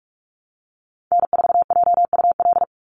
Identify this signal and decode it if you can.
Morse code, heard as a keyed tone, about 46 wpm, 700 Hz. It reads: N4JUL
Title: USA Call Signs 45wpm